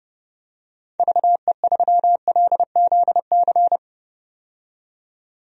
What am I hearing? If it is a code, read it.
Morse code, heard as a keyed tone, about 30 wpm, 700 Hz. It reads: VE3LZC
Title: DXCC-100 Call Signs - 6 Characters - 30wpm